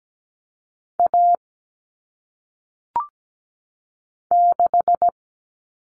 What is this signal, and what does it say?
Morse code, heard as a keyed tone, about 17 wpm, 700 Hz. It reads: A 6
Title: All Cut Numbers - 1-Digit 17wpm